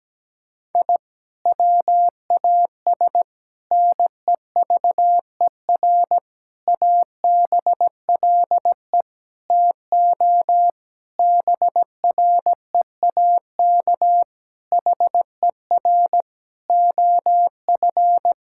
Morse code, heard as a keyed tone, about 17 wpm, 700 Hz. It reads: I WAS NEVER ABLE TO BREAK HER OF
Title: Sentences from Top 1000 Words - Review Part 6 - 17wpm